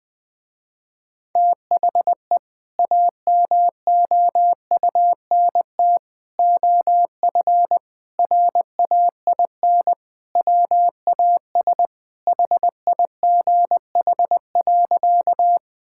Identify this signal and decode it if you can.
Morse code, heard as a keyed tone, about 19 wpm, 700 Hz. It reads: THE AMOUNT OF RAIN WAS HIGH.
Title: Sentences with 3-Word Phrases - Spoken Beginning - 20wpm